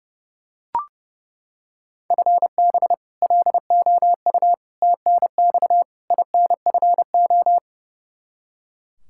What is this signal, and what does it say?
Morse code, heard as a keyed tone, about 30 wpm, 700 Hz. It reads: FB LOU TNX INFO
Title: K1USN SST QSO Practice 30wpm